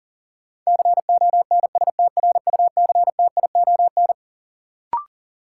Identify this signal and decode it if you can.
Morse code, heard as a keyed tone, about 40 wpm, 700 Hz. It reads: CONSTRUCTION
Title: Sentences from Top 1600 Words - New Word Drill Alt - 40wpm